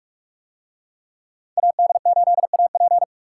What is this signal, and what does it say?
Morse code, heard as a keyed tone, about 39 wpm, 700 Hz. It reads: AD8AP